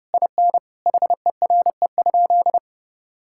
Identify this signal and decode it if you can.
Morse code, heard as a keyed tone, about 30 wpm, 700 Hz. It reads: IN HERE?
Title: Sentences from Top 1000 Words - Review Part 6 - 30wpm